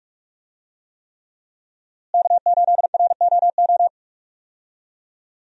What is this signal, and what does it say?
Morse code, heard as a keyed tone, about 42 wpm, 700 Hz. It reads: K8ROO
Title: USA Call Signs - 5 Characters - Repeated 3x - 45wpm